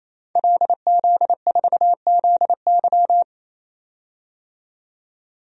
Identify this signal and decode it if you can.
Morse code, heard as a keyed tone, about 29 wpm, 690 Hz. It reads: LZ4ZY
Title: DXCC-100 Call Signs - 5 Characters - 28wpm